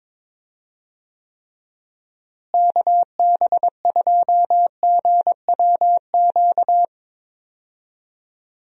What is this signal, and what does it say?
Morse code, heard as a keyed tone, about 21 wpm, 700 Hz. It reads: KB2GWQ
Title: USA Call Signs - 6 Characters - Repeated 3x - 22wpm